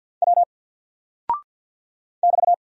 Morse code, heard as a keyed tone, about 48 wpm, 700 Hz. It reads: W <BT>
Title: Single Character (Letter:Number:Punctuation:Prosign) Alt 50wpm